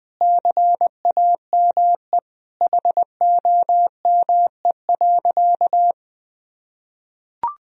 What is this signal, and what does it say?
Morse code, heard as a keyed tone, about 20 wpm, 700 Hz. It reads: CAME HOME.
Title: Sentences from Top 1000 Words - Review Part 4 - 20wpm